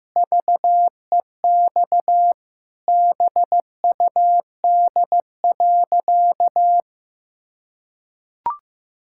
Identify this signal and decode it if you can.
Morse code, heard as a keyed tone, about 16 wpm, 700 Hz. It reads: VEX BUD.
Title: Pangram Challenge v2 - 15wpm